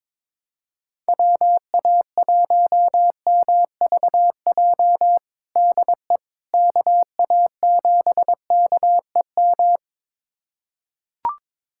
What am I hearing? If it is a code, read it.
Morse code, heard as a keyed tone, about 22 wpm, 700 Hz. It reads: WA1MVJ DE KA7KEM
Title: USA Callsign DE Callsign - 6 Characters - 22wpm